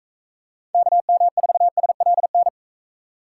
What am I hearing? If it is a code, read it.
Morse code, heard as a keyed tone, about 43 wpm, 700 Hz. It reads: KM4SLN